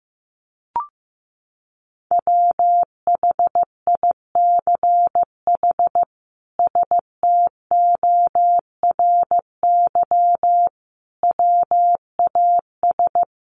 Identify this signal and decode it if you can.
Morse code, heard as a keyed tone, about 15 wpm, 700 Hz. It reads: WHICH STORY WAS
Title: Sentences from Top 300 Words 15wpm